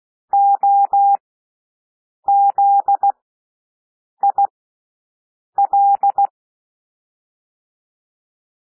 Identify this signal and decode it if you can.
Morse code, heard as a keyed tone, about 16 wpm, spaced as for 6 wpm, 800 Hz. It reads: OZIL
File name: cwmp3 / W1 SOUND WORDS 16 wpm